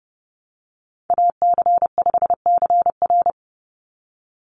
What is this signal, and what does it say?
Morse code, heard as a keyed tone, about 30 wpm, 700 Hz. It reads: AC5CR
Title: USA Call Signs 30wpm